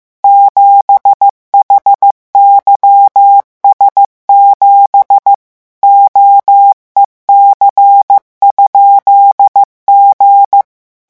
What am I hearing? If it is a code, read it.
Morse code, heard as a keyed tone, about 15 wpm, 800 Hz. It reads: 7HYS7 OEC?G
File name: morse mp3 / revision12-vitesse 14 mots